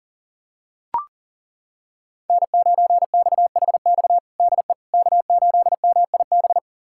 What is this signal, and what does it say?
Morse code, heard as a keyed tone, about 37 wpm, 700 Hz. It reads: N9XHX DE K8MIB
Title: USA Callsign DE Callsign - 5 Characters - 40wpm